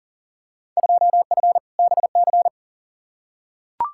Morse code, heard as a keyed tone, about 40 wpm, 700 Hz. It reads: 2F BC
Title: Field Day Exchanges Encore - 40wpm